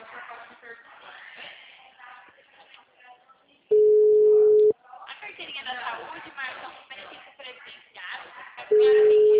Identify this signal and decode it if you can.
Ringback tone (Europe, 425 Hz), with speech.